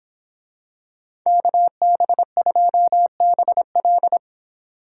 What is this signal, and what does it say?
Morse code, heard as a keyed tone, about 26 wpm, 700 Hz. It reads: KB2BL